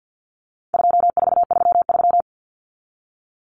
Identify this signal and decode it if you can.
Morse code, heard as a keyed tone, about 48 wpm, 700 Hz. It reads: 2433
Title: Common Cut Numbers - 4-Digit 50wpm